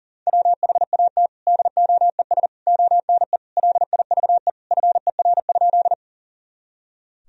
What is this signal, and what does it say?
Morse code, heard as a keyed tone, about 40 wpm, 700 Hz. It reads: WHAT DOES ONE LIVE FER?
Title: Sentences from Top 200 Words and Abbreviations 40wpm